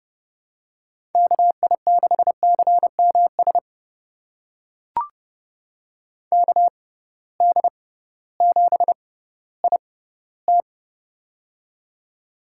Morse code, heard as a keyed tone, about 30 wpm, spaced as for 16 wpm, 700 Hz. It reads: KI6CMS KD7IT
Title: USA Call Signs - 6x Character Spacing - 30wpm